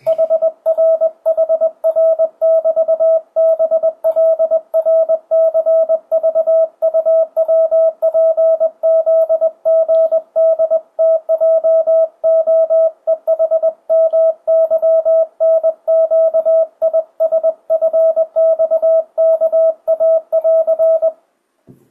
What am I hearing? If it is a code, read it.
Morse code, heard as a keyed tone, about 21 wpm, 650 Hz. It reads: HRHR<BT>BLRCVUWPZGDTJOEHMYNQISFXKA<AR>
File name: 2025cw / speedup / 21wpm